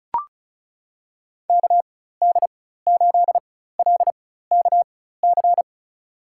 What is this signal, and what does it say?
Morse code, heard as a keyed tone, about 34 wpm, spaced as for 16 wpm, 700 Hz. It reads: KD8LKC
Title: USA Call Signs - 6 Characters - 4x Character Spacing - 35wpm